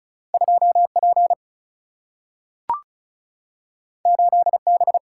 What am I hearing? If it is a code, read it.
Morse code, heard as a keyed tone, about 36 wpm, 700 Hz. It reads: 2P 8B
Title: Single-Letter Single-Number 35wpm